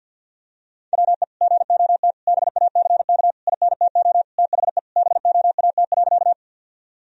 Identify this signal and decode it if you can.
Morse code, heard as a keyed tone, about 52 wpm, 700 Hz. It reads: WE GOT BACK INTO THE BOAT.
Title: Sentences from Top 1000 Words - Review Part 3 - 50wpm